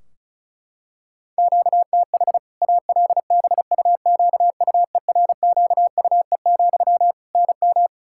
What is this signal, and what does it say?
Morse code, heard as a keyed tone, about 34 wpm, 700 Hz. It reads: QTH ALBUQUERQUE, NM
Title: Practice QSO - Exchange 1 - Beginning - 35wpm